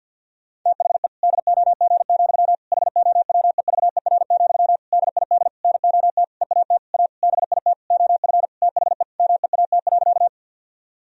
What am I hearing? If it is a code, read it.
Morse code, heard as a keyed tone, about 50 wpm, 700 Hz. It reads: THE DOG, HOWEVER, DID NOT EAT A BIT OF THE MEAT.
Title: Sentences from Top 1000 Words - Review Part 3 - 50wpm